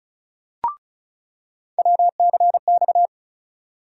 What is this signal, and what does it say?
Morse code, heard as a keyed tone, about 36 wpm, 700 Hz. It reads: WCX
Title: USA Call Sign Postfixes - 3 Letters 35wpm